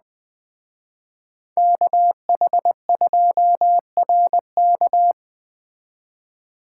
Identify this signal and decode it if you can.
Morse code, heard as a keyed tone, about 21 wpm, 700 Hz. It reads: KH2RK
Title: Speed Racing - USA 2x2 Call Signs 17wpm